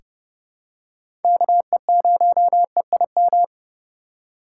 Morse code, heard as a keyed tone, about 30 wpm, 700 Hz. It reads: KE0EIM
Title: USA Call Signs - 2x Character Spacing - 30wpm